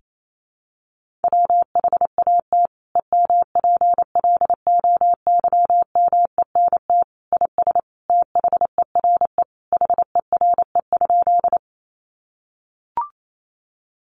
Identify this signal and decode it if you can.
Morse code, heard as a keyed tone, about 28 wpm, 700 Hz. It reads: WHAT EMPLOYMENT IS THERE HERE?
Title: Sentences from Top 1500 Words 28wpm